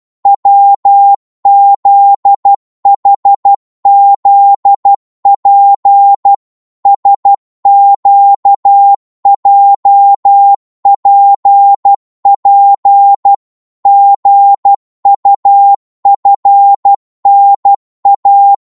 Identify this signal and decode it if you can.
Morse code, heard as a keyed tone, about 12 wpm, 800 Hz. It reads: WZHZP SQJPP GUFNA
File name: Morse / ltr-12wpm